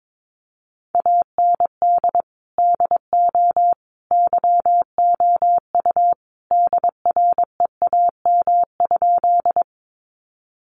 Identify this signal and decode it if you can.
Morse code, heard as a keyed tone, about 22 wpm, 700 Hz. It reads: AND DO YOU DREAM?
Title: Sentences from Top 1500 Words 22wpm